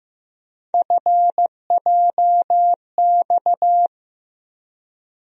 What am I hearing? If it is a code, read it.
Morse code, heard as a keyed tone, about 15 wpm, 700 Hz. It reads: FJX
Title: USA Call Sign Postfixes - 3 Letters 15wpm